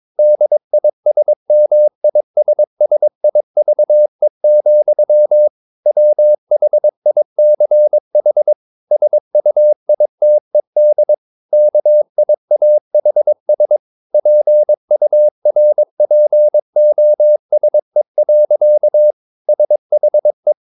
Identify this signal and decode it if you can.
Morse code, heard as a keyed tone, about 22 wpm, 600 Hz. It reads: DISMISSIVE, WHICH SUITED KIAHS PURPOSE. SHE
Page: Romanen Never af Ken Follet på engelsk.